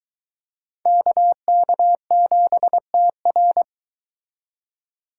Morse code, heard as a keyed tone, about 24 wpm, 700 Hz. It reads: KK7TR